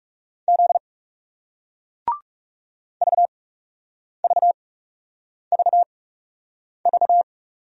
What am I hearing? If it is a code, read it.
Morse code, heard as a keyed tone, about 40 wpm, spaced as for 8 wpm, 700 Hz. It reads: Z VVVV